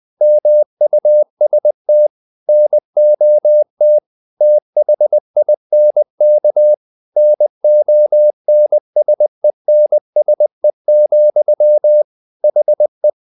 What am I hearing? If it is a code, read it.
Morse code, heard as a keyed tone, about 20 wpm, 600 Hz. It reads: MUST NOT THINK NONSENSE, HE
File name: the old man and the sea part 2 20wpm